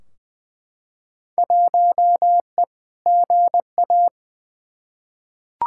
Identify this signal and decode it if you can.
Morse code, heard as a keyed tone, about 20 wpm, 700 Hz. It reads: 1E GA
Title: Field Day Exchanges - 20wpm